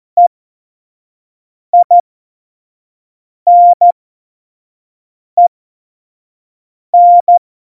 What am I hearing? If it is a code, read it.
Morse code, heard as a keyed tone, about 14 wpm, spaced as for 5 wpm, 700 Hz. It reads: EINEN
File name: Morsekurs / Witze / Witz 21 April 2020, 25 F70